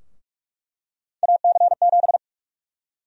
Morse code, heard as a keyed tone, about 46 wpm, 700 Hz. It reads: AC7
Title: USA Call Sign Prefixes - Repeated 2x - 45wpm